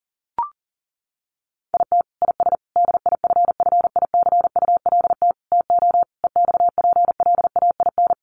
Morse code, heard as a keyed tone, about 40 wpm, 700 Hz. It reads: IT IS DIFFICULT TO EXPLAIN
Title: Sentences from Top 1100 Words 40wpm